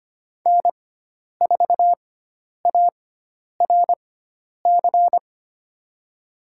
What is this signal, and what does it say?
Morse code, heard as a keyed tone, about 25 wpm, spaced as for 10 wpm, 700 Hz. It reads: N4ARC
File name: Lesson 16g - Letter c - Call Signs 25wpm